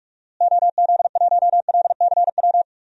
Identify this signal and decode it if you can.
Morse code, heard as a keyed tone, about 45 wpm, 700 Hz. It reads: OZ1LCW
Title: DXCC-100 Call Signs - 6 Characters - Repeated 2x - 45wpm